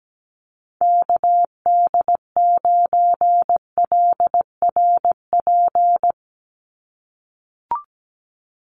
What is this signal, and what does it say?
Morse code, heard as a keyed tone, about 17 wpm, 700 Hz. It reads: KD9LRP
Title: USA Call Signs 17wpm